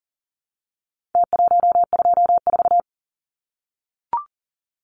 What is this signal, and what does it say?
Morse code, heard as a keyed tone, about 40 wpm, 700 Hz. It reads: T124
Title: Numbers - Contest Style - 4-Digit 40wpm